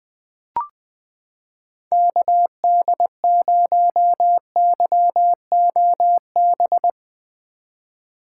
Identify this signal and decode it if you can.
Morse code, heard as a keyed tone, about 20 wpm, 700 Hz. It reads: KD0YOB
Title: 2023 SOTA Chaser Call Signs - 20wpm